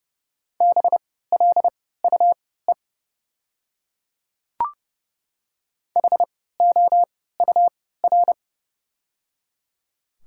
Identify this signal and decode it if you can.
Morse code, heard as a keyed tone, about 30 wpm, spaced as for 17 wpm, 690 Hz. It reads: BLUE HOUR
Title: 3x Character Spacing - Top 500 Words - 30wpm